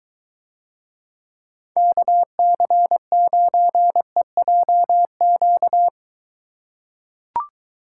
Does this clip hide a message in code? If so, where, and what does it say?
Yes: Morse code from 1.8 to 5.9 s, heard as a keyed tone, about 22 wpm, 700 Hz. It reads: KC9EJQ